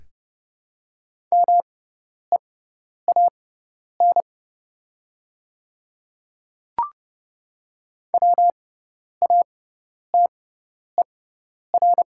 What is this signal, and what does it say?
Morse code, heard as a keyed tone, about 30 wpm, spaced as for 10 wpm, 690 Hz. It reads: MEAN WATER
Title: 6x Character Spacing - Top 200 Words - 30wpm